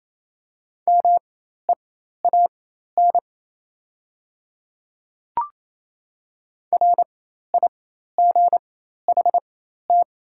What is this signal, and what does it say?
Morse code, heard as a keyed tone, about 28 wpm, spaced as for 13 wpm, 690 Hz. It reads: MEAN RIGHT
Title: 4x Character Spacing - Top 200 Words - 28wpm